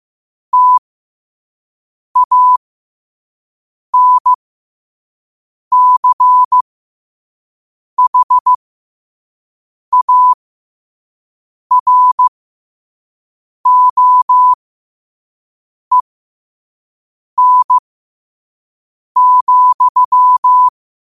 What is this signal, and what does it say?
Morse code, heard as a keyed tone, about 15 wpm, spaced as for 5 wpm, 1000 Hz. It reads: TANCHAROEN,
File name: qotdfiles / 5wpm